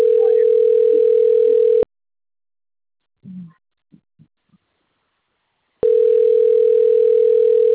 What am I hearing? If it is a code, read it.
Ringback tone (North America), 440 + 480 Hz.